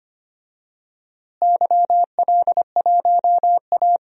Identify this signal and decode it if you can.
Morse code, heard as a keyed tone, about 25 wpm, 700 Hz. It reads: YL1A
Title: DXCC-100 Call Signs - 4 Characters - Repeated 3x - 25wpm